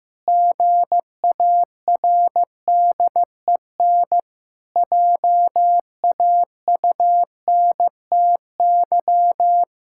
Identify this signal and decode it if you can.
Morse code, heard as a keyed tone, about 15 wpm, 700 Hz. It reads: GARDEN JAUNTY
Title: Pangram Challenge v2 - 15wpm